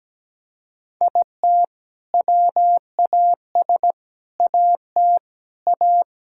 Morse code, heard as a keyed tone, about 17 wpm, 700 Hz. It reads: IT WAS AT A
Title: Sentences from Top 1600 Words 17wpm